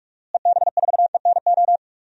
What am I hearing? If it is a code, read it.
Morse code, heard as a keyed tone, about 46 wpm, 700 Hz. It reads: EB4ENO
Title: DXCC-100 Call Signs - 6 Characters - Repeated 2x - 45wpm